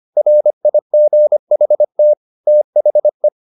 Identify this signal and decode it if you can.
Morse code, heard as a keyed tone, about 25 wpm, 600 Hz. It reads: RIGHT THE